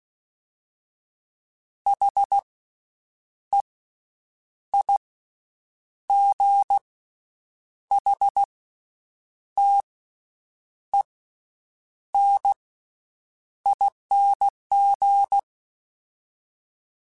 Morse code, heard as a keyed tone, about 16 wpm, spaced as for 7 wpm, 800 Hz. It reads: HEIGHTENING
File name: Special INGs / Special INGs-mp3s / VerySlowToFaster Ing @15